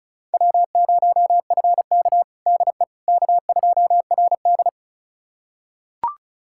Morse code, heard as a keyed tone, about 36 wpm, 700 Hz. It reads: W0FK DE K2RD
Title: USA Callsign DE Callsign - 4 Characters - 35wpm